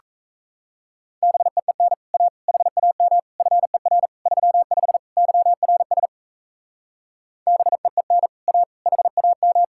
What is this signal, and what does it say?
Morse code, heard as a keyed tone, about 37 wpm, 700 Hz. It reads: BEEN A HAM FER 35 YRS BEEN A HAM